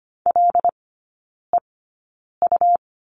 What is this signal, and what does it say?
Morse code, heard as a keyed tone, about 25 wpm, spaced as for 9 wpm, 700 Hz. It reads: LEU
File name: Lesson 17e - Letter u - Three Characters 25wpm